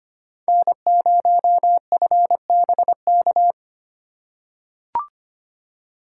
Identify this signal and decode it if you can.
Morse code, heard as a keyed tone, about 25 wpm, 700 Hz. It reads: N0FBK
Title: USA Call Signs - 5 Characters - 3x Character Spacing - 25wpm